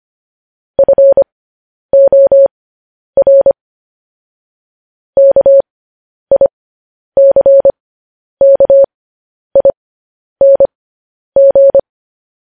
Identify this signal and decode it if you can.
Morse code, heard as a keyed tone, about 25 wpm, spaced as for 10 wpm, 550 Hz. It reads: FOR KICKING
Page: news-headlines-in-morse-code-at-10-wpm